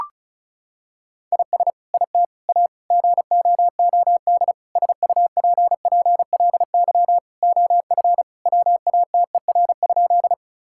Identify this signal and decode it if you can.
Morse code, heard as a keyed tone, about 36 wpm, 700 Hz. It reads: IS IT A GOOD SUPPLY OF WATER?
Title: Sentences from Top 1000 Words - Review Part 5 - 35wpm